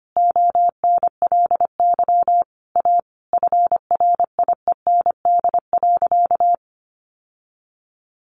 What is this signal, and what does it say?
Morse code, heard as a keyed tone, about 25 wpm, 700 Hz. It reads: ONLY A FRIEND.
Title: Sentences from Top 1000 Words 25wpm